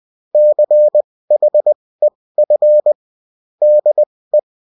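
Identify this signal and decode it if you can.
Morse code, heard as a keyed tone, about 20 wpm, spaced as for 16 wpm, 600 Hz. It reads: CHEF DE
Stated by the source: Bulletin in text and CW form.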